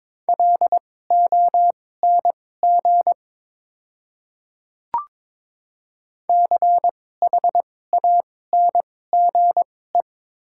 Morse code, heard as a keyed tone, about 22 wpm, spaced as for 16 wpm, 700 Hz. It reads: LONG CHANGE
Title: 2x Character Spacing - Top 300 Words - 22wpm